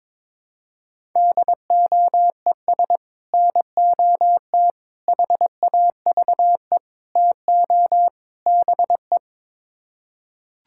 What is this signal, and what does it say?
Morse code, heard as a keyed tone, about 22 wpm, 700 Hz. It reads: DOES NOT HAVE TO BE
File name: practice / 22wpm / Build A Phrase - 5-Word Phrases - 22wpm